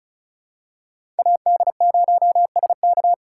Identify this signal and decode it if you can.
Morse code, heard as a keyed tone, about 34 wpm, 700 Hz. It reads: AD0SK
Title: USA Call Signs - 2x Character Spacing - 35wpm